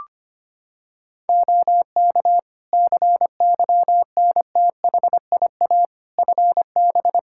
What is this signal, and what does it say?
Morse code, heard as a keyed tone, about 25 wpm, 700 Hz. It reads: OK CYNTHIA FB
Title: K1USN SST QSO Practice 25wpm